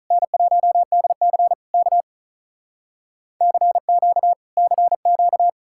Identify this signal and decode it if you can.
Morse code, heard as a keyed tone, about 47 wpm, 700 Hz. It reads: N1DC K CQ CQ